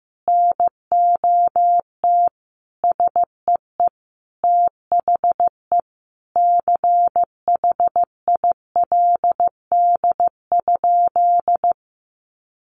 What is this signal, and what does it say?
Morse code, heard as a keyed tone, about 15 wpm, 700 Hz. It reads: NOT SEE THE CHILD?
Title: Sentences from Top 800 Words 15wpm